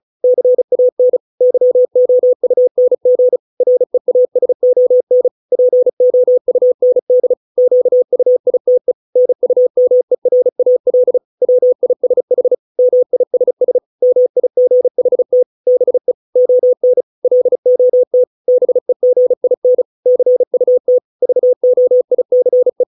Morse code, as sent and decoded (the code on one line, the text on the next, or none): -.-. .- -. / -.-- --- ..- -. --. / .-. .- ... --- -. / .--. --- ..- -. -.. / --.- ..- .. - . / -. ..- -- . .-. .- .-.. / .-- .. ... .... / -- .. ... ... / -- .. --. .... - / -... / --- -. / .-.. --- - / -... --. .. -. / -.-. ..- - / ...- --- .. -.-. .
CAN YOUNG REASON POUND QUITE NUMERAL WISH MISS MIGHT BE ON LOT BEGIN CUT VOICE